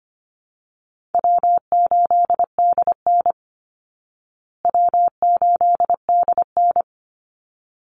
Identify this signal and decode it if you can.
Morse code, heard as a keyed tone, about 25 wpm, 700 Hz. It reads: W8DN W8DN
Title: Easy Call Signs Repeated 2x - 25wpm